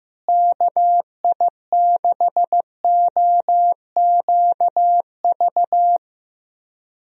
Morse code, heard as a keyed tone, about 15 wpm, 700 Hz. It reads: KI6OQV
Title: USA Call Signs - 6 Characters - Repeated 2x - 15wpm